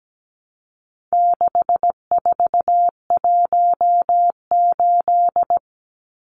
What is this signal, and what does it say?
Morse code, heard as a keyed tone, about 17 wpm, 700 Hz. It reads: 6418
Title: Numbers - 4-Digit 17wpm